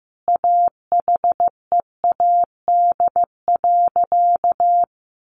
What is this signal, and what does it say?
Morse code, heard as a keyed tone, about 15 wpm, 700 Hz. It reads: AHEAD.
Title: Sentences from Top 1400 Words 15wpm